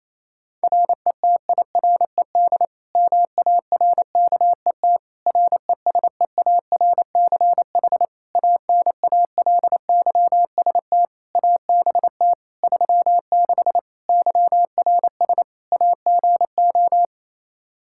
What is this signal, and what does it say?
Morse code, heard as a keyed tone, about 27 wpm, 700 Hz. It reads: RETIRED MARKET RESEARCH ANALYST ABT 36 YRS AGO
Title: Practice QSO - Exchange 3 - More Info - 28wpm